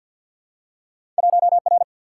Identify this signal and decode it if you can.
Morse code, heard as a keyed tone, about 48 wpm, 700 Hz. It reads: 1R
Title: Single-Letter Single-Number 50wpm